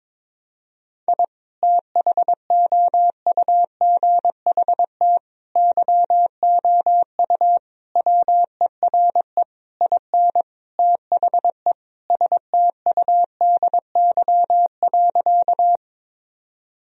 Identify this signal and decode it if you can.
Morse code, heard as a keyed tone, about 22 wpm, 700 Hz. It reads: I THOUGHT YOU WERE IN THE STUDY.
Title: Sentences from Top 1000 Words - Review Part 1 - 22wpm